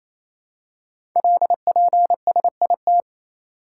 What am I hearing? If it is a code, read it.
Morse code, heard as a keyed tone, about 28 wpm, 700 Hz. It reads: LPSIT
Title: Code Groups - 5 Letters - 28wpm